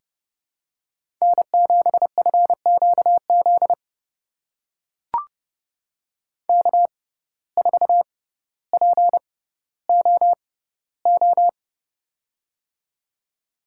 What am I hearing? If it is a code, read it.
Morse code, heard as a keyed tone, about 32 wpm, spaced as for 16 wpm, 700 Hz. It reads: N7FQZ K4POO
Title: USA Call Signs - 5 Characters - 6x Character Spacing - 30wpm